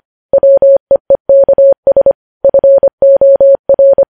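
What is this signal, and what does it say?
Morse code, heard as a keyed tone, about 25 wpm, 550 Hz. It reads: WEEKS FOR
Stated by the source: Morse code news headlines sent at 25 words per minute.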